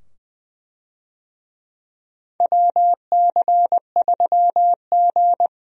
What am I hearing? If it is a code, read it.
Morse code, heard as a keyed tone, about 20 wpm, 700 Hz. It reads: WC3G
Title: USA Call Signs - 4 Characters - 20wpm